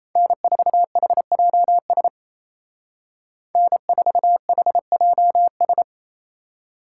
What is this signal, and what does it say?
Morse code, heard as a keyed tone, about 34 wpm, 700 Hz. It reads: N4HJS N4HJS
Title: Speed Racing - USA 1x3 Call Signs 28wpm